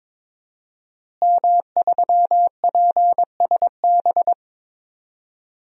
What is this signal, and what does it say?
Morse code, heard as a keyed tone, about 21 wpm, 700 Hz. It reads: M3PSB
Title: DXCC-100 Call Signs - 5 Characters - Repeated 3x - 22wpm